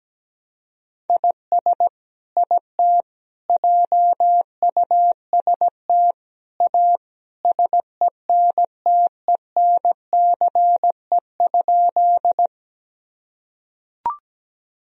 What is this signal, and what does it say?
Morse code, heard as a keyed tone, about 17 wpm, 700 Hz. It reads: IS IT JUST A SENTENCE?
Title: Sentences from Top 1000 Words - Review Part 1 - 17wpm